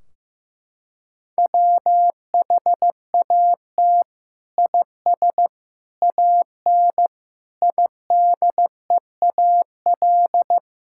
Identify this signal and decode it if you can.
Morse code, heard as a keyed tone, about 15 wpm, 700 Hz. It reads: WHAT IS AN IDEAL
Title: Sentences from Top 1600 Words 15wpm